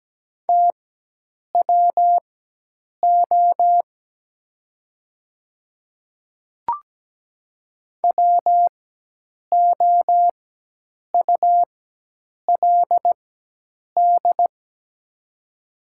Morse code, heard as a keyed tone, about 17 wpm, spaced as for 8 wpm, 700 Hz. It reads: TWO WOULD
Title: 4x Character Spacing - Top 200 Words - 17wpm